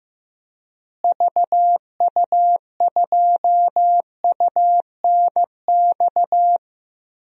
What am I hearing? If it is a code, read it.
Morse code, heard as a keyed tone, about 15 wpm, 700 Hz. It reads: VU2UNX
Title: DXCC-100 Call Signs - 6 Characters - Repeated 3x - 15wpm